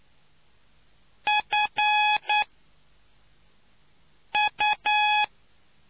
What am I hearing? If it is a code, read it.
Morse code, heard as a keyed tone, about 9 wpm, spaced as for 4 wpm, 840 Hz. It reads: FU